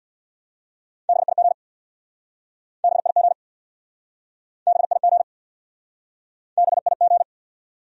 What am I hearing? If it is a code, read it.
Morse code, heard as a keyed tone, about 65 wpm, 700 Hz. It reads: BIG BIG BIG BIG